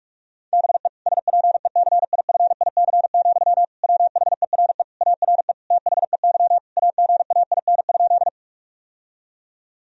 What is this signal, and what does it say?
Morse code, heard as a keyed tone, about 46 wpm, 700 Hz. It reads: BE SPECIFIC, WHERE ARE THEY AGAIN?
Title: Sentences from Top 1000 Words - Review Part 3 - 45wpm